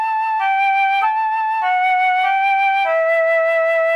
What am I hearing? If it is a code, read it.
There is no code.